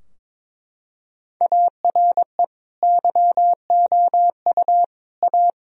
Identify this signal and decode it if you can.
Morse code, heard as a keyed tone, about 22 wpm, 700 Hz. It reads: ARE YOU A
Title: Sentences from Top 1000 Words - Review Part 4 - 22wpm